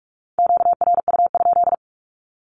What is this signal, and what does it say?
Morse code, heard as a keyed tone, about 44 wpm, 700 Hz. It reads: QRU?
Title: Q-Codes 45wpm